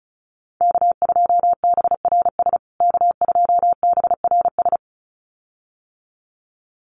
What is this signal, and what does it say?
Morse code, heard as a keyed tone, about 34 wpm, 700 Hz. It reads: K2BRS K2BRS
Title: USA Call Signs Repeated 2x 35wpm